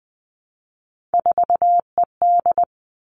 Morse code, heard as a keyed tone, about 20 wpm, 700 Hz. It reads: 4ED
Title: Three Challenging Letter-Number Combinations 20wpm